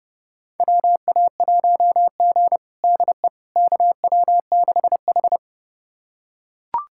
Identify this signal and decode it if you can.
Morse code, heard as a keyed tone, about 32 wpm, 700 Hz. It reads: WA1G DE KW6H
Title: USA Callsign DE Callsign - 4 Characters - 30wpm